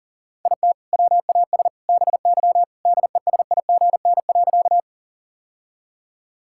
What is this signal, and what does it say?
Morse code, heard as a keyed tone, about 40 wpm, 700 Hz. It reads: IT WAS BY DESIGN.
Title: Sentences from Top 1000 Words - Review Part 5 - 40wpm